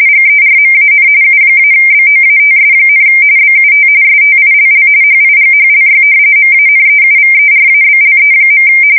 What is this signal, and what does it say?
RTTY, about 45.45 baud. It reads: GERMAN U-BOATS, BUT STILL HIGHLY MANEUVERABLE.
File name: fsk170 / usnavy4